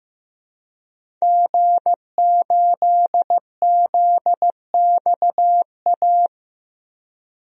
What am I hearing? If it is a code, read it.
Morse code, heard as a keyed tone, about 15 wpm, 700 Hz. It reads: G8ZXA
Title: DXCC-100 Call Signs - 5 Characters - Repeated 3x - 15wpm